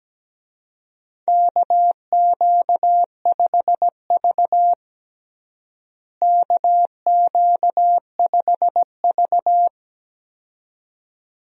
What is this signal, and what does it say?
Morse code, heard as a keyed tone, about 17 wpm, 700 Hz. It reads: KQ5V KQ5V
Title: USA Call Signs - 4 Characters - Repeated 2x - 17wpm